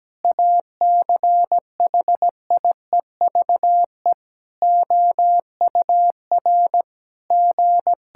Morse code, heard as a keyed tone, about 17 wpm, 700 Hz. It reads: ACHIEVE OUR G
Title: Sentences from Top 1000 Words - Review Part 6 - 17wpm